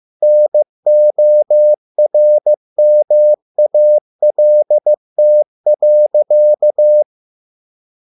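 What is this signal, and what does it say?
Morse code, heard as a keyed tone, about 15 wpm, 600 Hz. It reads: NORMALT.
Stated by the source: Morse code at a slow 15 wpm from CW med Gnister.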